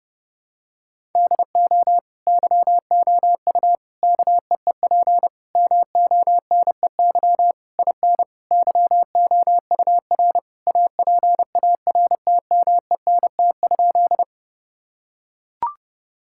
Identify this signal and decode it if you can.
Morse code, heard as a keyed tone, about 32 wpm, 700 Hz. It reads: DO YOU KEEP MONEY IN YOUR APARTMENT?
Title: Sentences from Top 1000 Words - Review Part 6 - 30wpm